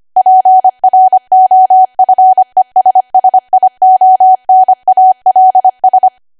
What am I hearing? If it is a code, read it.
Morse code, heard as a keyed tone, about 25 wpm, 750 Hz. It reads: PROFESSIONALS
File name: Archive / 25 WPM / 25WPM